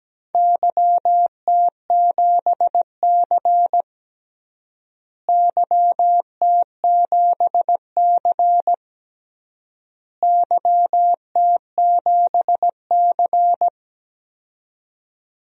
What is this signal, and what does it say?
Morse code, heard as a keyed tone, about 17 wpm, 700 Hz. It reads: YT7C YT7C YT7C
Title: DXCC-100 Call Signs - 4 Characters - Repeated 3x - 17wpm